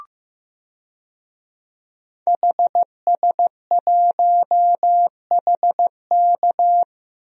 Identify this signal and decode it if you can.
Morse code, heard as a keyed tone, about 15 wpm, 700 Hz. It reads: HS1HK
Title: DXCC-100 Call Signs - 5 Characters - 15wpm